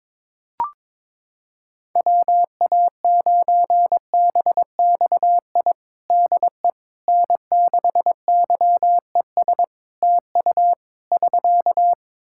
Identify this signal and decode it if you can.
Morse code, heard as a keyed tone, about 22 wpm, 700 Hz. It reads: WA9BXI DE N6YES TU <SK>
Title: Practice QSO - Exchange 4 - Ending - 22wpm